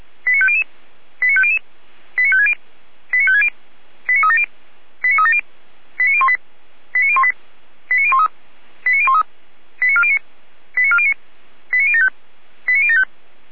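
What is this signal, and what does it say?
Five-tone selective call (ZVEI), heard as quick runs of five tones, tones of about 70 ms. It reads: 89500 89500 89560 89560 89370 89370 89018 89018 89013 89013 89409 89409 89075 89075
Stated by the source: ZVEI five-tone call received on an SDR.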